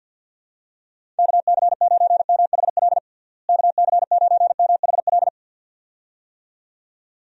Morse code, heard as a keyed tone, about 51 wpm, 700 Hz. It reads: KC9MHL KC9MHL
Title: USA Call Signs - 6 Characters - Repeated 2x - 50wpm